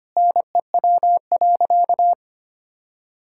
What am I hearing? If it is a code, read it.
Morse code, heard as a keyed tone, about 25 wpm, 700 Hz. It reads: NEW.
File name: practice / 25wpm / Difficult Sentences with Hints - 25wpm